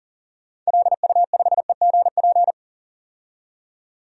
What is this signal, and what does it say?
Morse code, heard as a keyed tone, about 40 wpm, 700 Hz. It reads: LU5EGP
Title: DXCC-100 Call Signs - 6 Characters - 40wpm